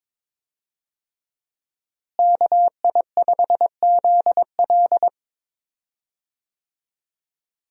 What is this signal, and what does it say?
Morse code, heard as a keyed tone, about 21 wpm, 700 Hz. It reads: KI5ZL
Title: USA Call Signs - 5 Characters - Repeated 2x - 22wpm